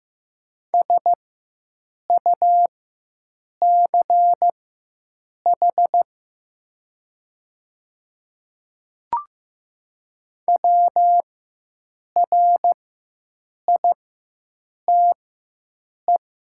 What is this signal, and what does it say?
Morse code, heard as a keyed tone, about 15 wpm, spaced as for 7 wpm, 700 Hz. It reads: SUCH WRITE
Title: 4x Character Spacing - Top 200 Words - 15wpm